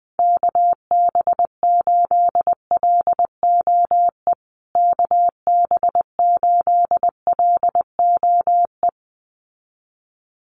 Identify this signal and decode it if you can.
Morse code, heard as a keyed tone, about 20 wpm, 700 Hz. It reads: KB8LOE KB8LOE
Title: USA Call Signs Repeated 2x 20wpm